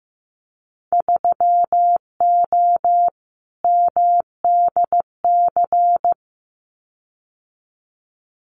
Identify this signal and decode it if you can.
Morse code, heard as a keyed tone, about 15 wpm, 700 Hz. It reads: 3O MDC
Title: Winter Field Day Exchanges Encore 15wpm